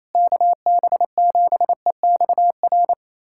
Morse code, heard as a keyed tone, about 28 wpm, 700 Hz. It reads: KB7EXR
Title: USA Callsign DE Callsign 28wpm